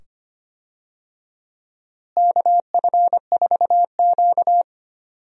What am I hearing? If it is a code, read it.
Morse code, heard as a keyed tone, about 25 wpm, 700 Hz. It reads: KF4Q
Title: USA Call Signs - 4 Characters - Repeated 2x - 25wpm